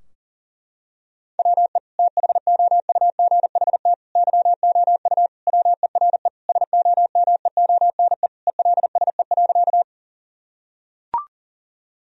Morse code, heard as a keyed tone, about 40 wpm, 700 Hz. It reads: WE THOUGHT YOU WERE SOMEONE ELSE.
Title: Sentences from Top 1000 Words - Review Part 4 - 40wpm